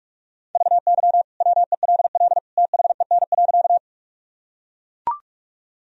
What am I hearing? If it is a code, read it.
Morse code, heard as a keyed tone, about 42 wpm, 700 Hz. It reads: VY WELL THEN.
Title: Sentences from Top 200 Words and Abbreviations 45wpm